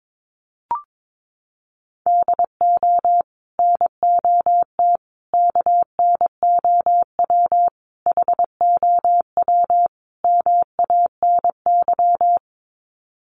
Morse code, heard as a keyed tone, about 22 wpm, 700 Hz. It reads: DO NOT KNOW HOW MANY
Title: Sets of 5 Words Encore 22wpm